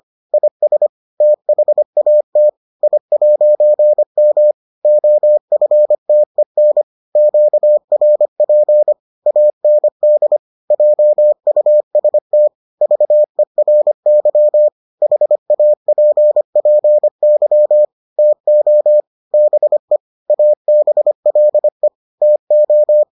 Morse code, heard as a keyed tone, about 25 wpm, 600 Hz. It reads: IS THAT I'M OFTEN QRP AND JUST VERY HAPPY TO BE ABLE TO